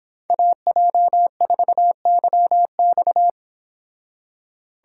Morse code, heard as a keyed tone, about 26 wpm, 700 Hz. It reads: AJ4YX